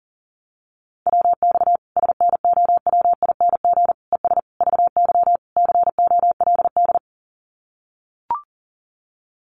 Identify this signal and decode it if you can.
Morse code, heard as a keyed tone, about 40 wpm, 700 Hz. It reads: WX SNOWING ES VY COLD
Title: Common QSO Phrases 40wpm